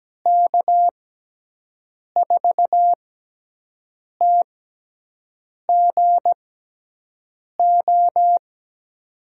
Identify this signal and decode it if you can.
Morse code, heard as a keyed tone, about 17 wpm, spaced as for 6 wpm, 700 Hz. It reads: K4TGO
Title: USA Call Signs - 6 Characters - 6x Character Spacing - 17wpm